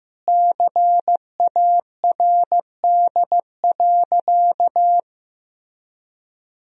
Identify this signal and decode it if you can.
Morse code, heard as a keyed tone, about 15 wpm, 700 Hz. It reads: CARD.
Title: Sentences from Top 1000 Words - Review Part 4 - 15wpm